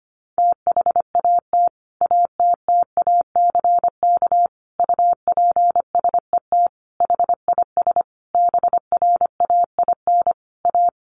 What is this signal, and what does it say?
Morse code, heard as a keyed tone, about 25 wpm, 700 Hz. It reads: THAT ATTACK UPSET HIS BRAIN A
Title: Sentences from Top 1500 Words 25wpm